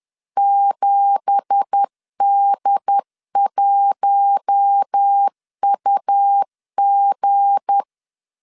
Morse code, heard as a keyed tone, about 10 wpm, 800 Hz. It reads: 7D1UG
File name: LEZIONI / 9 2867 10wpm 2733